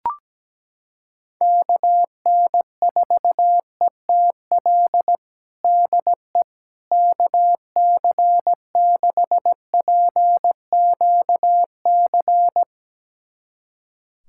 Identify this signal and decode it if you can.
Morse code, heard as a keyed tone, about 17 wpm, 700 Hz. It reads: KN4ETL DE KC6PQC
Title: USA Callsign DE Callsign 17wpm